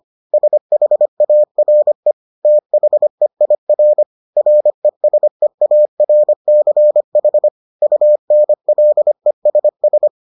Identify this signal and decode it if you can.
Morse code, heard as a keyed tone, about 25 wpm, 600 Hz. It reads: SHARE THEIR RESEARCH UNLESS